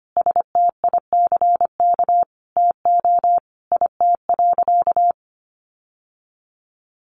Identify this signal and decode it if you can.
Morse code, heard as a keyed tone, about 25 wpm, 700 Hz. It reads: STICK TO IT.
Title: Sentences from Top 1200 Words 25wpm